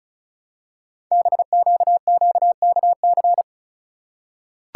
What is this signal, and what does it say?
Morse code, heard as a keyed tone, about 36 wpm, 690 Hz. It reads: BQQKC
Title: Code Groups - 5 Letters - 35wpm